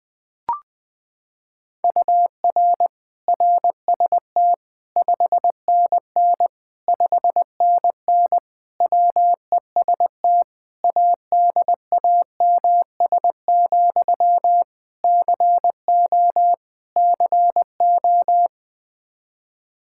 Morse code, heard as a keyed tone, about 21 wpm, 700 Hz. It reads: UR RST 5NN 5NN WEST ADAMS, CO CO
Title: Practice QSO - Exchange 1 - Beginning - 20wpm